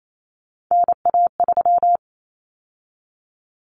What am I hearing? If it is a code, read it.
Morse code, heard as a keyed tone, about 28 wpm, 700 Hz. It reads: NA3
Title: Common Cut Numbers - 3-Digit 28wpm